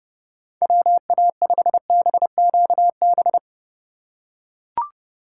Morse code, heard as a keyed tone, about 30 wpm, 700 Hz. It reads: WA5BQB
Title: USA Call Signs - 6 Characters - 2x Character Spacing - 30wpm